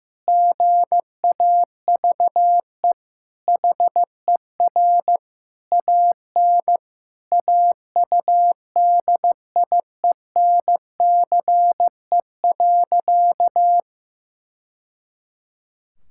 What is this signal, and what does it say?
Morse code, heard as a keyed tone, about 15 wpm, 700 Hz. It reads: GAVE HER AN AUDIENCE.
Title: Sentences from Top 1000 Words - Review Part 5 - 15wpm